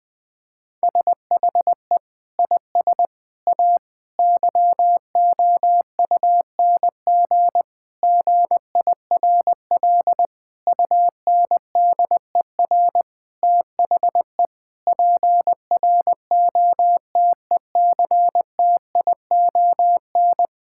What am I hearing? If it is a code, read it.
Morse code, heard as a keyed tone, about 20 wpm, 700 Hz. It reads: SHE IS A YOUNG GIRL UNDER THE PROTECTION
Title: Sentences from Top 1600 Words 20wpm